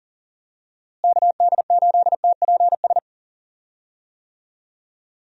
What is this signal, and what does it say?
Morse code, heard as a keyed tone, about 39 wpm, 700 Hz. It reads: KD8TPS
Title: USA Call Signs - 6 Characters - Repeated 2x - 40wpm